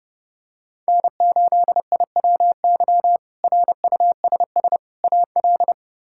Morse code, heard as a keyed tone, about 32 wpm, 700 Hz. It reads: N8IWY RUSS AL
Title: K1USN SST QSO Practice - 30wpm